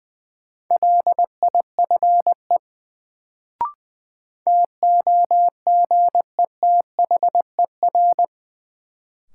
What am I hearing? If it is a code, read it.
Morse code, heard as a keyed tone, about 20 wpm, 700 Hz. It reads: LIFE TOGETHER
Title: Sentences from Top 300 Words - New Word Drill - 20wpm